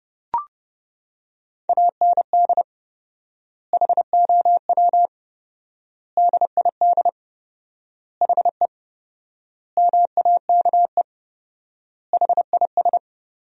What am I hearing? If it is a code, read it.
Morse code, heard as a keyed tone, about 30 wpm, 700 Hz. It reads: AND HOW DID HE MAKE HIS
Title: 4x Word-Spacing - Sentences from Top 400 Words - 30wpm